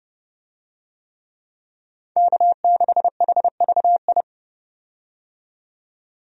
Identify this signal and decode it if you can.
Morse code, heard as a keyed tone, about 30 wpm, 700 Hz. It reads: K6HVI
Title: USA Call Signs - 5 Characters - 30wpm